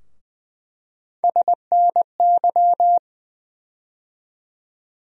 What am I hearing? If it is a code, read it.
Morse code, heard as a keyed tone, about 20 wpm, 700 Hz. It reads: SNY
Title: USA Call Sign Postfixes - 3 Letters 20wpm